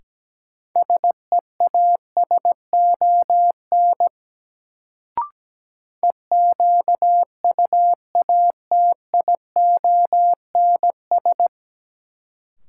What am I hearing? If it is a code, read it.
Morse code, heard as a keyed tone, about 17 wpm, 700 Hz. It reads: SEASON EQUATIONS
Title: Sets of 5 Words - New Word Drill - 17wpm